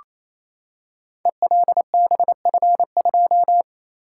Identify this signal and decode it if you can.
Morse code, heard as a keyed tone, about 27 wpm, 700 Hz. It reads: ELBF2
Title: Code Groups - 5 Letter:Numbers - 28wpm